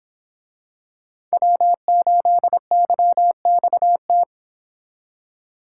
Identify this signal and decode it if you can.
Morse code, heard as a keyed tone, about 26 wpm, 700 Hz. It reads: W8YXT
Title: Speed Racing - USA 1x3 Call Signs 22wpm